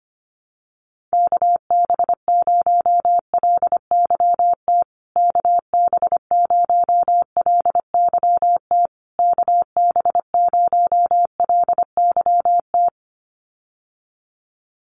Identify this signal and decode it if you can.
Morse code, heard as a keyed tone, about 25 wpm, 700 Hz. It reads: KB0LYT KB0LYT KB0LYT
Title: USA Call Signs Repeated 3x 25wpm